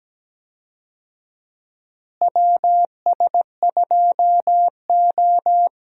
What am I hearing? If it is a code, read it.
Morse code, heard as a keyed tone, about 17 wpm, 700 Hz. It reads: WS2O
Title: USA Call Signs - 4 Characters - Repeated 3x - 17wpm